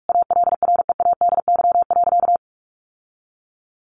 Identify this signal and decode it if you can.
Morse code, heard as a keyed tone, about 44 wpm, 700 Hz. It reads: ALREADY.
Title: Sentences from Top 1000 Words 45wpm